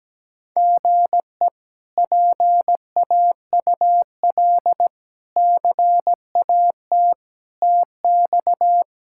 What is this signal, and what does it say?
Morse code, heard as a keyed tone, about 17 wpm, 700 Hz. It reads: GE PAUL CAT TX
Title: K1USN SST QSO Practice - 17wpm